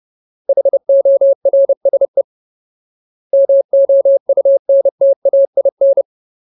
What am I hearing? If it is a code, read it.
Morse code, heard as a keyed tone, about 30 wpm, 550 Hz. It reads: HORSE MOUNTAIN
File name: Ultimate Word List / 30WPM Ultimate